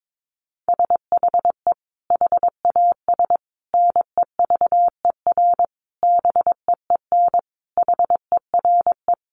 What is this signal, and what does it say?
Morse code, heard as a keyed tone, about 22 wpm, 700 Hz. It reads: SHE HAS NEVER BEEN HERE
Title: Additional Easy Phrases 22wpm